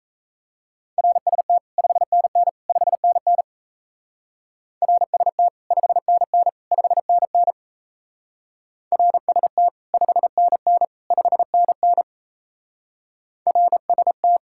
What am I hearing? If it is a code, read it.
Morse code, heard as a keyed tone, about 36 wpm, 700 Hz. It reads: RST 5NN 5NN RST 5NN 5NN RST 5NN 5NN RST